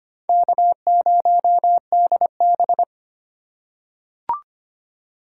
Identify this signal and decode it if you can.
Morse code, heard as a keyed tone, about 25 wpm, 700 Hz. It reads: K0DB
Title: USA Call Signs - 4 Characters - 2x Character Spacing - 25wpm